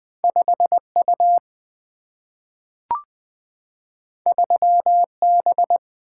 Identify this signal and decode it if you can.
Morse code, heard as a keyed tone, about 20 wpm, 700 Hz. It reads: 5 U 3 B
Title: Single-Letter Single-Number 20wpm